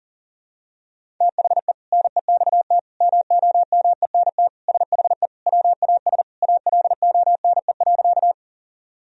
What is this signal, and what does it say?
Morse code, heard as a keyed tone, about 40 wpm, 700 Hz. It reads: THE NEXT MOMENT SHE WAS ALONE.
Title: Sentences from Top 1000 Words - Review Part 1 - 40wpm